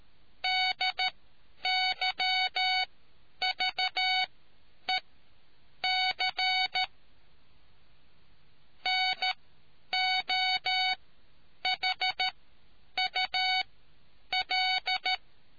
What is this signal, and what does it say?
Morse code, heard as a keyed tone, about 13 wpm, spaced as for 8 wpm, 2250 Hz. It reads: DYVEC NOHUL